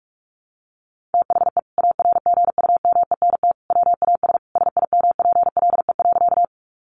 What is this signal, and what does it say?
Morse code, heard as a keyed tone, about 44 wpm, 700 Hz. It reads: THE ARGUMENT WAS SIMPLE.
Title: Sentences from Top 1400 Words 45wpm